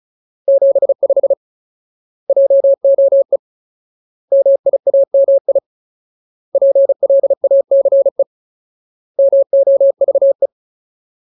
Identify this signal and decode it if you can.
Morse code, heard as a keyed tone, about 34 wpm, 550 Hz. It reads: 75 JOE MIAMI PLACE MOVE
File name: Ultimate Word List / 35WPM Ultimate